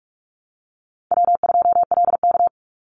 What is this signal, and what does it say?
Morse code, heard as a keyed tone, about 44 wpm, 700 Hz. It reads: W2LK
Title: Call Signs Repeated 2x - 45wpm